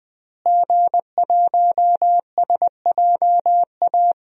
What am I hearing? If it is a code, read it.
Morse code, heard as a keyed tone, about 20 wpm, 700 Hz. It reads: G1SJA
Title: DXCC-100 Call Signs - 5 Characters - Repeated 2x - 20wpm